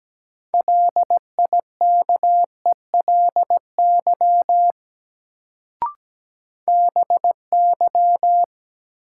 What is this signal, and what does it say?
Morse code, heard as a keyed tone, about 17 wpm, 700 Hz. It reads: LIKELY BY
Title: Sets of 2 Words - New Word Drill - 17wpm